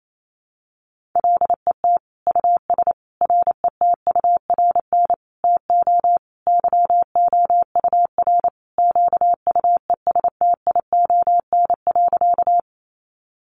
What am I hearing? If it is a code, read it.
Morse code, heard as a keyed tone, about 28 wpm, 700 Hz. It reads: LET US RETURN TO YOUR QUESTION.
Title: Sentences from Top 800 Words 28wpm